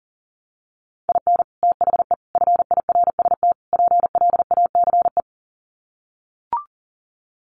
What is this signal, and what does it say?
Morse code, heard as a keyed tone, about 40 wpm, 700 Hz. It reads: IN THE FIRST PLACE
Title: Sets of 4 Words 40wpm